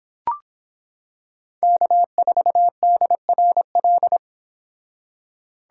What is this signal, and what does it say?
Morse code, heard as a keyed tone, about 25 wpm, 700 Hz. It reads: K4DRL